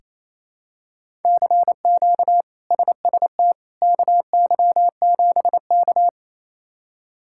Morse code, heard as a keyed tone, about 28 wpm, 700 Hz. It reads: CQ SST KY7K
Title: K1USN SST QSO Practice - 28wpm